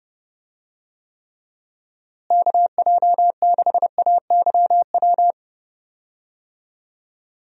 Morse code, heard as a keyed tone, about 30 wpm, 700 Hz. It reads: KJ6AYW
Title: USA Call Signs - 6 Characters - Repeated 3x - 30wpm